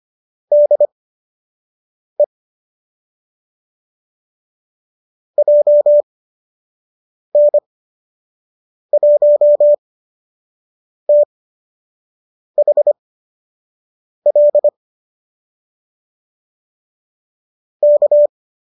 Morse code, heard as a keyed tone, about 25 wpm, spaced as for 6 wpm, 590 Hz. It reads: DE JN1THL K
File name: qso003 06